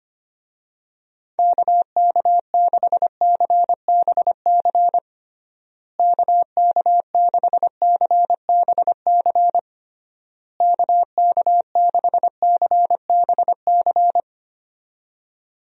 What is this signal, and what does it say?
Morse code, heard as a keyed tone, about 25 wpm, 700 Hz. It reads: KK6CBC KK6CBC KK6CBC
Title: USA Call Signs - 6 Characters - Repeated 3x - 25wpm